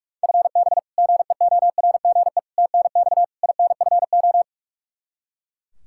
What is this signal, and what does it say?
Morse code, heard as a keyed tone, about 42 wpm, 700 Hz. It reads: FB GEORGE TNX INFO
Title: K1USN SST QSO Practice 45wpm